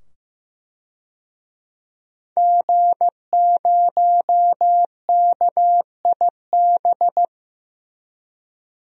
Morse code, heard as a keyed tone, about 16 wpm, 700 Hz. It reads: G0KIB
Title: DXCC-100 Call Signs - 5 Characters - 15wpm